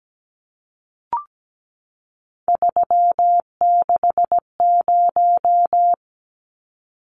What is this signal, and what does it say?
Morse code, heard as a keyed tone, about 17 wpm, 700 Hz. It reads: 360
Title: Numbers - 3-Digit 17wpm